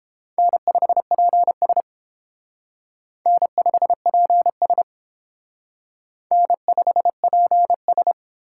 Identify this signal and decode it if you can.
Morse code, heard as a keyed tone, about 32 wpm, 700 Hz. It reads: N5PS N5PS N5PS